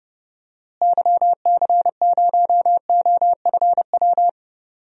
Morse code, heard as a keyed tone, about 30 wpm, 700 Hz. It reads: YC0OFW
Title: DXCC-100 Call Signs - 6 Characters - Repeated 3x - 30wpm